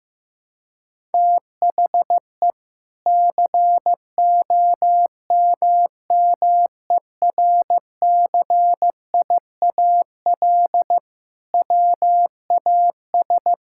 Morse code, heard as a keyed tone, about 15 wpm, 700 Hz. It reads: THE COMMERCIAL WAS
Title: Sentences from Top 1000 Words - Review Part 5 - 15wpm